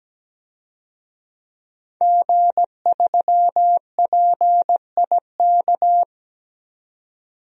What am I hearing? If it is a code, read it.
Morse code, heard as a keyed tone, about 17 wpm, 700 Hz. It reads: G3PIK
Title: DXCC-100 Call Signs - 5 Characters - 17wpm